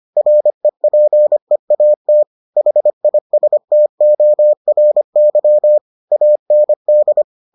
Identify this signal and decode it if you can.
Morse code, heard as a keyed tone, about 25 wpm, 600 Hz. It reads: REPEAT HISTORY AND